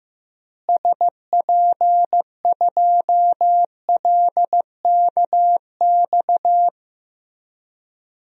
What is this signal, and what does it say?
Morse code, heard as a keyed tone, about 15 wpm, 700 Hz. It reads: SP2LKX
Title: DXCC-100 Call Signs - 6 Characters - Repeated 3x - 15wpm